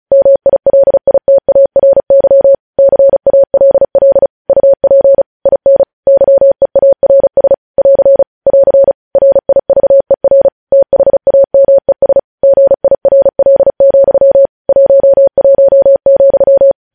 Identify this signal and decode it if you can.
Morse code, heard as a keyed tone, about 36 wpm, 550 Hz. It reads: MILITARY CALL UP IN YEARS <AR> <AR> RIVER THAMES GIRL, 11,
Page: news headlines in morse code at 35 wpm